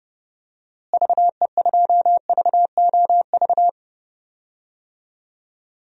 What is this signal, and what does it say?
Morse code, heard as a keyed tone, about 30 wpm, 700 Hz. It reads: VE2VOV
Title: DXCC-100 Call Signs - 6 Characters - Repeated 3x - 30wpm